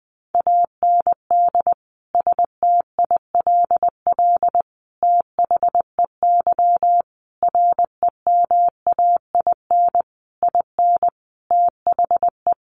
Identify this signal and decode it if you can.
Morse code, heard as a keyed tone, about 20 wpm, 700 Hz. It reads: AND STILL THEY REMAIN IN THE
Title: Sentences from Top 1200 Words 20wpm